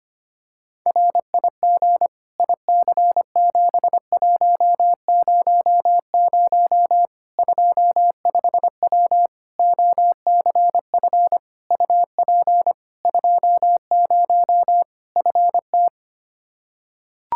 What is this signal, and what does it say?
Morse code, heard as a keyed tone, about 25 wpm, 700 Hz. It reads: RIG IC7100 25W OCF UP 20 FT
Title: K1USN SST QSO Practice 25wpm